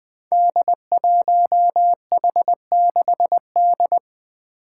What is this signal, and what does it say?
Morse code, heard as a keyed tone, about 20 wpm, 700 Hz. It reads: D1H6D
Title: Five Challenging Letter-Number Combinations 20wpm